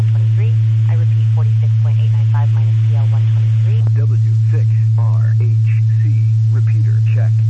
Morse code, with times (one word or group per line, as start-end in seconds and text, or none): none